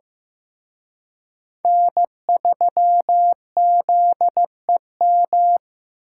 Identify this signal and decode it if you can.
Morse code, heard as a keyed tone, about 18 wpm, 700 Hz. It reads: N3ZEM
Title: Speed Racing - USA 1x3 Call Signs 15wpm